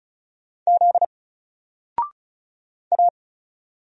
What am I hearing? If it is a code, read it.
Morse code, heard as a keyed tone, about 36 wpm, 700 Hz. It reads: Z A
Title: Single Character (Letter:Number:Punctuation:Prosign) Alt 35wpm